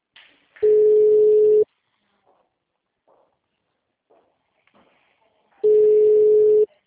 Ringback tone (Europe, 425 Hz).